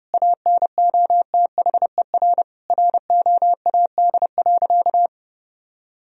Morse code, heard as a keyed tone, about 32 wpm, 700 Hz. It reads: ANOTHER ROAD.
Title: Sentences from Top 1000 Words - Review Part 2 - 30wpm